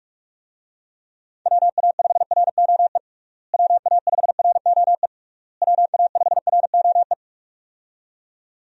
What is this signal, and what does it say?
Morse code, heard as a keyed tone, about 47 wpm, 700 Hz. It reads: WA5ROE WA5ROE WA5ROE
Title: USA Call Signs - 6 Characters - Repeated 3x - 45wpm